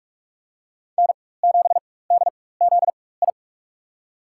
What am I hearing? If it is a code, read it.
Morse code, heard as a keyed tone, about 42 wpm, spaced as for 21 wpm, 700 Hz. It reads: N7DZI
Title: USA Call Signs - 5 Characters - 4x Character Spacing - 45wpm